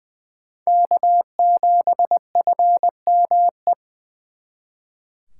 Morse code, heard as a keyed tone, about 24 wpm, 700 Hz. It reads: K7FME
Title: Speed Racing - USA Call Signs 20wpm